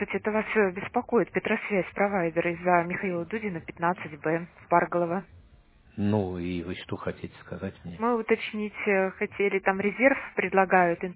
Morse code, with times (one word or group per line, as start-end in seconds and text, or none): none